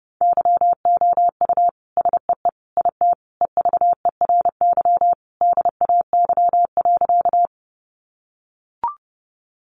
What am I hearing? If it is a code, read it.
Morse code, heard as a keyed tone, about 30 wpm, 700 Hz. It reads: YOU SEE IT EVERY DAY.
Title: Sentences from Top 200 Words 30wpm